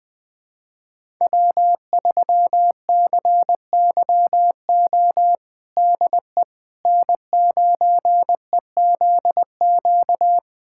Morse code, heard as a keyed tone, about 21 wpm, 700 Hz. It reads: W3CYO DE N9EZQ
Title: USA Callsign DE Callsign - 5 Characters - 20wpm